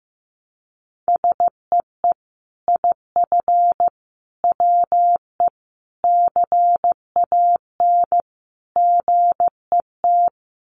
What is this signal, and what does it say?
Morse code, heard as a keyed tone, about 15 wpm, 700 Hz. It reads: SEE IF WE CAN GET
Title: Sets of 5 Words Encore2 15wpm